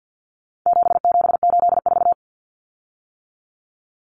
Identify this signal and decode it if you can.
Morse code, heard as a keyed tone, about 50 wpm, 700 Hz. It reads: 7784
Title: Common Cut Numbers - 4-Digit 50wpm